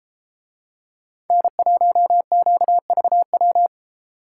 Morse code, heard as a keyed tone, about 31 wpm, 700 Hz. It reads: N1QVW